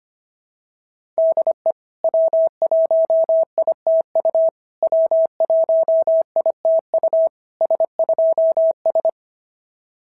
Morse code, heard as a keyed tone, about 25 wpm, 660 Hz. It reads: DE W1ITU W1ITU S2S
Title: SOTA QSO Practice 25wpm